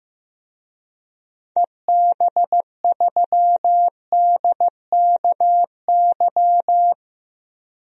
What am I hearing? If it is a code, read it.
Morse code, heard as a keyed tone, about 16 wpm, 700 Hz. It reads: EB3DKY
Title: DXCC-100 Call Signs - 6 Characters - 15wpm